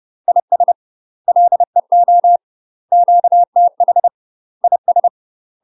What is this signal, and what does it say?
Morse code, heard as a keyed tone, about 30 wpm, 700 Hz. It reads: IS LEO QTH IS
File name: UK MASTER QSOs TRAINING/30WPM